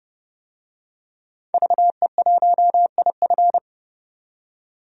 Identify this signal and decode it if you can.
Morse code, heard as a keyed tone, about 32 wpm, 700 Hz. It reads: VE1IF
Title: DXCC-100 Call Signs - 5 Characters - 30wpm